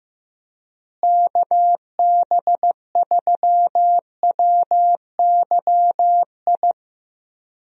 Morse code, heard as a keyed tone, about 15 wpm, 700 Hz. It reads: KB3WYI
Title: USA Call Signs - 6 Characters - Repeated 3x - 15wpm